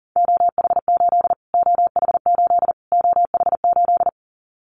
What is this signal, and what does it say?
Morse code, heard as a keyed tone, about 40 wpm, 700 Hz. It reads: OH8 OH8 OH8
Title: Callsign Prefixes for Top 25 DXCC - Repeated 3x - 40wpm